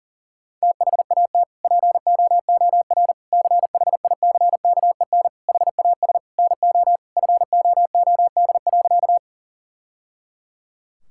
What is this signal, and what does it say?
Morse code, heard as a keyed tone, about 40 wpm, 700 Hz. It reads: THAT POOR CHICKEN HAS NO FOOD.
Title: Sentences from Top 1000 Words - Review Part 5 - 40wpm